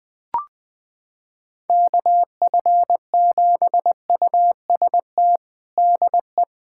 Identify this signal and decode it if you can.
Morse code, heard as a keyed tone, about 20 wpm, 700 Hz. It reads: KF7UST DE
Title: USA Callsign DE Callsign 20wpm